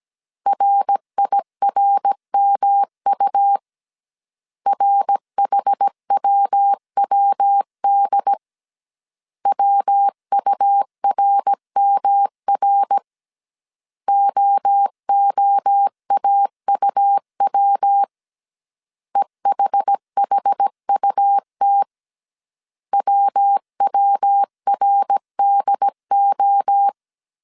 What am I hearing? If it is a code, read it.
Morse code, heard as a keyed tone, about 17 wpm, 800 Hz. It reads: RIRMU RHWWD WURMR OOAUW EHHUT WWRDO